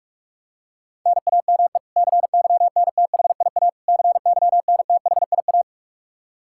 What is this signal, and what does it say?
Morse code, heard as a keyed tone, about 46 wpm, 700 Hz. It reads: NAME CYNTHIA CYNTHIA
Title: K1USN SST QSO Practice 45wpm